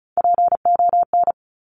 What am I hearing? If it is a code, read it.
Morse code, heard as a keyed tone, about 36 wpm, 700 Hz. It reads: PON
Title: Top Three Letter Combinations in Words 35wpm